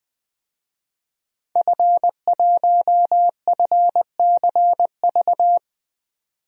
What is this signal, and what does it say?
Morse code, heard as a keyed tone, about 20 wpm, 700 Hz. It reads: F1FCV
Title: DXCC-100 Call Signs - 5 Characters - 20wpm